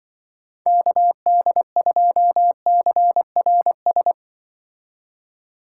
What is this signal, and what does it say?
Morse code, heard as a keyed tone, about 24 wpm, 700 Hz. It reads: KD2CRS
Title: Speed Racing - USA 2x3 Call Signs 20wpm